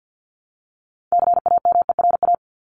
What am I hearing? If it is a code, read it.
Morse code, heard as a keyed tone, about 50 wpm, 700 Hz. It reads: CAMERA
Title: Top 501-750 Words 50wpm